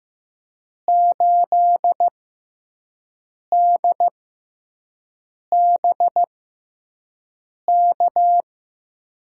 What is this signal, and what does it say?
Morse code, heard as a keyed tone, about 15 wpm, spaced as for 5 wpm, 700 Hz. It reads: 8DBK
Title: USA Call Signs - 6x Character Spacing - 15wpm